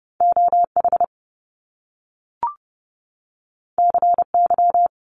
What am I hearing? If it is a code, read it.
Morse code, heard as a keyed tone, about 30 wpm, 700 Hz. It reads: OH CY
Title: Top Two Letter Combinations in Words 30wpm